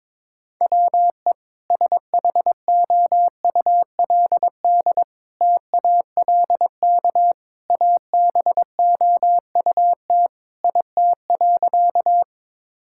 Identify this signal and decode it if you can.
Morse code, heard as a keyed tone, about 22 wpm, 700 Hz. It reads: WE SHOULD TALK ABOUT IT.
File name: practice / 22wpm / Sentences with 2-Word Phrases - 22wpm